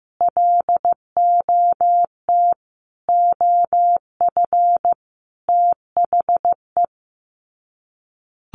Morse code, heard as a keyed tone, about 15 wpm, 700 Hz. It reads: LOT OF THE
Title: Sets of 4 Words 15wpm